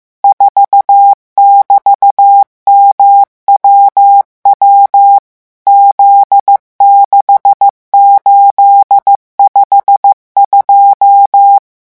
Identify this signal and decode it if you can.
Morse code, heard as a keyed tone, about 15 wpm, 800 Hz. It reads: 4<BT>MWW Z6852
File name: morse mp3 / revision13-vitesse 14 mots